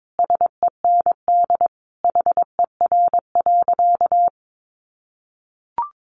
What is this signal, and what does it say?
Morse code, heard as a keyed tone, about 22 wpm, 700 Hz. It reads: SEND HER.
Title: Sentences from Top 1500 Words 22wpm